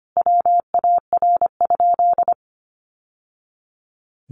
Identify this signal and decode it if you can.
Morse code, heard as a keyed tone, about 25 wpm, 700 Hz. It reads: WAR?
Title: Sentences from Top 1200 Words 25wpm